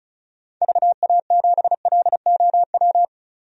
Morse code, heard as a keyed tone, about 36 wpm, 700 Hz. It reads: VA7LOW
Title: DXCC-100 Call Signs - 6 Characters - Repeated 3x - 35wpm